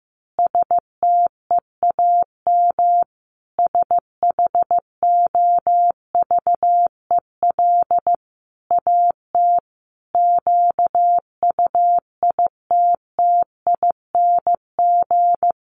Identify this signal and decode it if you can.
Morse code, heard as a keyed tone, about 15 wpm, 700 Hz. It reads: STEAM SHOVEL AT QUITTING
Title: A Sound Of Thunder 15wpm